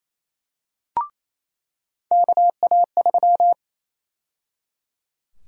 Morse code, heard as a keyed tone, about 28 wpm, 700 Hz. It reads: KA3
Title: USA Call Sign Prefixes - 28wpm